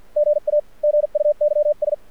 Morse code, heard as a keyed tone, about 52 wpm, 600 Hz. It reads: GA GUYS
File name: archived / CRSnet